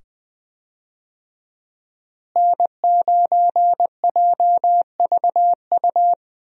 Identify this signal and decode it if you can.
Morse code, heard as a keyed tone, about 20 wpm, 700 Hz. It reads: N9JVU
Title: USA Call Signs - 5 Characters - Repeated 2x - 20wpm